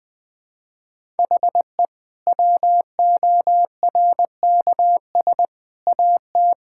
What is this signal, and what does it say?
Morse code, heard as a keyed tone, about 20 wpm, 700 Hz. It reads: HE WORKS AT
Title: Sentences with 8-Word Phrases - 20wpm